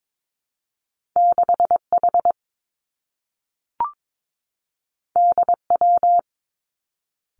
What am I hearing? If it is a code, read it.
Morse code, heard as a keyed tone, about 22 wpm, 700 Hz. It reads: 6 H D W
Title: Two Challenging Letter-Number Combinations 22wpm